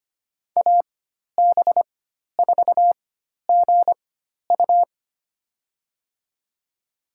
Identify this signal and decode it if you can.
Morse code, heard as a keyed tone, about 25 wpm, spaced as for 12 wpm, 700 Hz. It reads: AB4GU
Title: USA Call Signs - 5 Characters - 4x Character Spacing - 25wpm